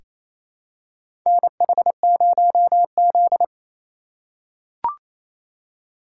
Morse code, heard as a keyed tone, about 29 wpm, 700 Hz. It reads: NH0Z
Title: USA Call Signs - 4 Characters - 3x Character Spacing - 28wpm